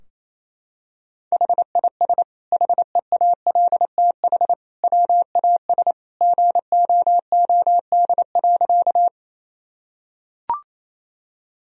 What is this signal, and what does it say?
Morse code, heard as a keyed tone, about 28 wpm, 700 Hz. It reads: HIS HEALTH WAS GOOD.
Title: Sentences from Top 1000 Words - Review Part 3 - 28wpm